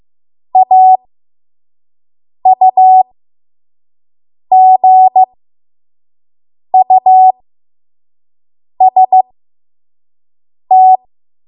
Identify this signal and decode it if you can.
Morse code, heard as a keyed tone, about 15 wpm, spaced as for 5 wpm, 750 Hz. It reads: AUGUST
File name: Morse / 05WPM